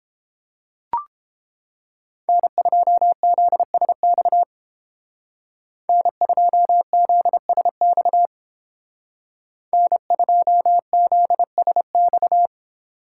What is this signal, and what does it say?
Morse code, heard as a keyed tone, about 29 wpm, 700 Hz. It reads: N2ZSX N2ZSX N2ZSX